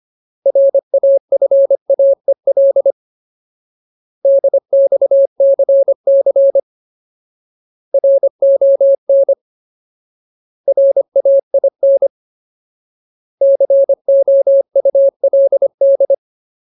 Morse code, heard as a keyed tone, about 25 wpm, 550 Hz. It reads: RAFAEL DXCC RON RAIN COULD
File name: Ultimate Word List / 25WPM Ultimate